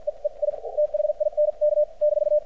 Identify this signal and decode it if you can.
Morse code, heard as a keyed tone, about 54 wpm, spaced as for 45 wpm, 600 Hz. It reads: E E I THINK -
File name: archived / CRSnet